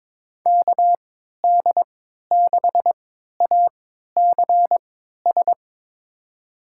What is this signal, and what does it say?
Morse code, heard as a keyed tone, about 22 wpm, spaced as for 12 wpm, 700 Hz. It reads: KD6ACS
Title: USA Call Signs - 6 Characters - 3x Character Spacing - 22wpm